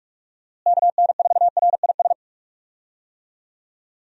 Morse code, heard as a keyed tone, about 41 wpm, 700 Hz. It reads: KN4RIS
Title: USA Call Signs - 6 Characters - Repeated 3x - 45wpm